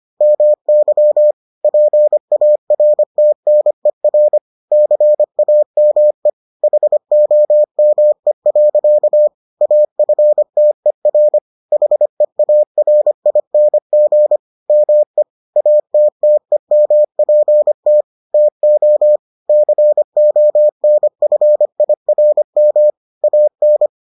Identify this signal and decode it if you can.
Morse code, heard as a keyed tone, about 25 wpm, 600 Hz. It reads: MY PARTNER CAME HOME. AFTER HEARING ME ATTEMPT TO CONFIRM AN